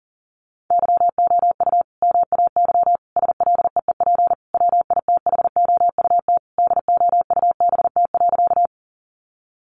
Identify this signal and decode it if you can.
Morse code, heard as a keyed tone, about 40 wpm, 700 Hz. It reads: YOU MAY SLEEP WITHOUT DOUBT.
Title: Sentences from Top 1500 Words 40wpm